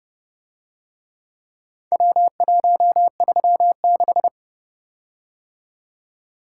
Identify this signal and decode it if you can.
Morse code, heard as a keyed tone, about 30 wpm, 700 Hz. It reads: W136